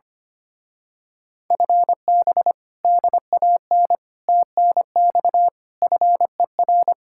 Morse code, heard as a keyed tone, about 24 wpm, 700 Hz. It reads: FB DAN TNX FER
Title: Speed-Racing Common QSO Phrases Encore - 25wpm